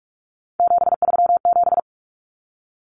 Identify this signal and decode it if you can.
Morse code, heard as a keyed tone, about 44 wpm, 700 Hz. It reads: 737
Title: Common Cut Numbers - 3-Digit 45wpm